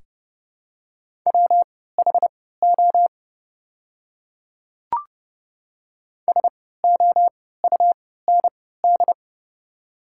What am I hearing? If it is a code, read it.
Morse code, heard as a keyed tone, about 32 wpm, spaced as for 17 wpm, 700 Hz. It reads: WHO SOUND
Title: 3x Character Spacing - Top 300 Words - 30wpm